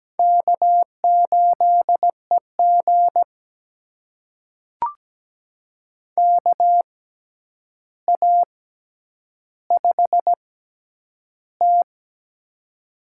Morse code, heard as a keyed tone, about 17 wpm, spaced as for 9 wpm, 700 Hz. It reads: K8EG KA5T
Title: USA Call Signs - 4 Characters - 6x Character Spacing - 17wpm